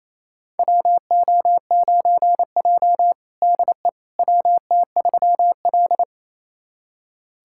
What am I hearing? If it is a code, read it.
Morse code, heard as a keyed tone, about 28 wpm, 700 Hz. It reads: WO9J DE WT3L
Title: USA Callsign DE Callsign - 4 Characters - 28wpm